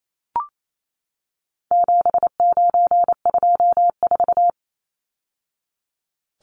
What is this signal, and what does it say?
Morse code, heard as a keyed tone, about 28 wpm, 700 Hz. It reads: 7924
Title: Numbers - 4-Digit 28wpm